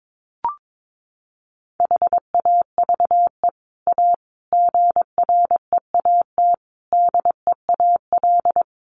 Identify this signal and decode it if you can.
Morse code, heard as a keyed tone, about 22 wpm, 700 Hz. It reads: HAVE A GREAT DEAL
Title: Sets of 5 Words Encore2 22wpm